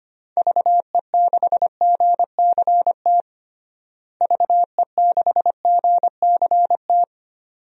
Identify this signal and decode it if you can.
Morse code, heard as a keyed tone, about 25 wpm, 700 Hz. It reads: VE6GCT VE6GCT
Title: DXCC-100 Call Signs - 6 Characters - Repeated 2x - 25wpm